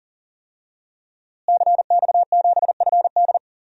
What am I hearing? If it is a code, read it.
Morse code, heard as a keyed tone, about 40 wpm, 700 Hz. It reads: CX7FD
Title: DXCC-100 Call Signs - 5 Characters - Repeated 2x - 40wpm